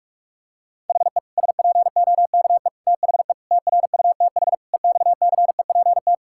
Morse code, heard as a keyed tone, about 46 wpm, 700 Hz. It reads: HE SPOKE THE TRUTH EXCEPT
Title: Sentences from Top 1600 Words 45wpm